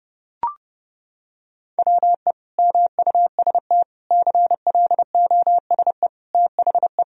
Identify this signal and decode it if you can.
Morse code, heard as a keyed tone, about 30 wpm, 700 Hz. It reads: WE MUST CLOSE THE
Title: Sentences from Top 1000 Words - Review Part 1 - 30wpm